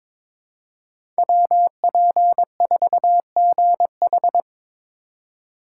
Morse code, heard as a keyed tone, about 22 wpm, 700 Hz. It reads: WP4GH
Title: USA Call Signs - 4x Character Spacing - 22wpm